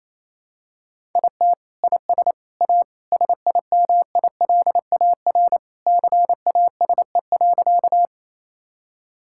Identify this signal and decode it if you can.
Morse code, heard as a keyed tone, about 28 wpm, 700 Hz. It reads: IT IS A SIMILAR CASE.
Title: Sentences from Top 1000 Words - Review Part 5 - 28wpm